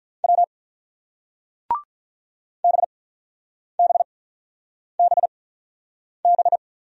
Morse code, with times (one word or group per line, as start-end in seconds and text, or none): 0.2-0.4 W
2.6-6.6 BBBB